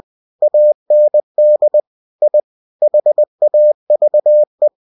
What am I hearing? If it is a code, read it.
Morse code, heard as a keyed tone, about 20 wpm, 600 Hz. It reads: AND I HAVE
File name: cw-books / winnie-the-pooh / winnie-20wpm-page-0009